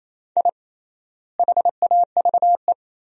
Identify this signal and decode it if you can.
Morse code, heard as a keyed tone, about 28 wpm, 700 Hz. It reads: I HAVE
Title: 3x Word-Spacing - Sentences from Top 400 Words - 28wpm